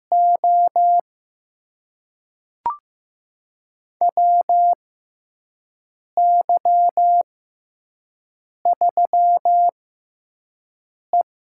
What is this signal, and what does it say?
Morse code, heard as a keyed tone, about 15 wpm, spaced as for 5 wpm, 700 Hz. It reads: O WY3E
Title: USA Call Signs - 4 Characters - 6x Character Spacing - 15wpm